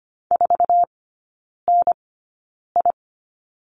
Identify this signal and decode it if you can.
Morse code, heard as a keyed tone, about 25 wpm, spaced as for 9 wpm, 700 Hz. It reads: 4NI
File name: Lesson 14e - Number 2 - Three Characters 25wpm